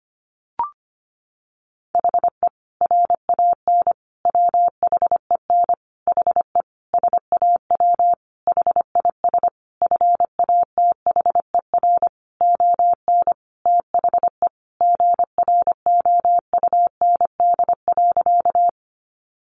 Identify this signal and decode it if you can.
Morse code, heard as a keyed tone, about 25 wpm, 700 Hz. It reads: HE RAN WHEN HE SAW HIS FATHER ON THE GROUND.
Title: Sentences from Top 500 Words 25wpm